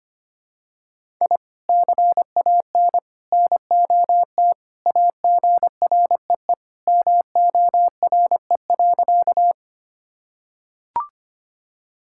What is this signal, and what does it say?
Morse code, heard as a keyed tone, about 25 wpm, 700 Hz. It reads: I CAN NOT AGREE MORE.
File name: practice / 25wpm / Sentences with 3-Word Phrases - 25wpm